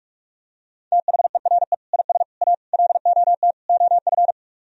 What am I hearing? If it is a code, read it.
Morse code, heard as a keyed tone, about 38 wpm, 700 Hz. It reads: THERE IS A LOT OF